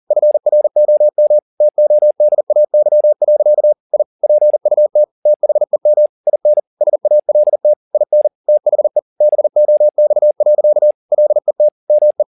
Morse code, heard as a keyed tone, about 40 wpm, 600 Hz. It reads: FROM TODAY. I PUT THEM IN SALT IN THE BOX. LET ME
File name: the old man and the sea 1 40wpm